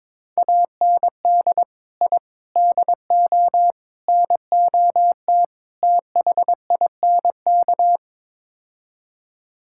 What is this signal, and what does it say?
Morse code, heard as a keyed tone, about 22 wpm, 700 Hz. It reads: AND I DO NOT THINK
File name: practice / 22wpm / Build A Phrase - 5-Word Phrases - 22wpm